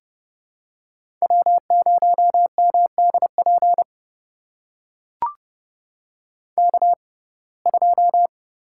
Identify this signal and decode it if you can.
Morse code, heard as a keyed tone, about 30 wpm, spaced as for 22 wpm, 700 Hz. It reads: W0MDP K2
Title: USA Call Signs - 5 Characters - 6x Character Spacing - 30wpm